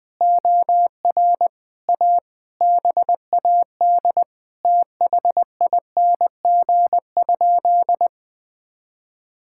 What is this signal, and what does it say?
Morse code, heard as a keyed tone, about 20 wpm, 700 Hz. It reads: OR A BAD THING?
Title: Sentences with 7-Word Phrases - Spoken Beginning - 20wpm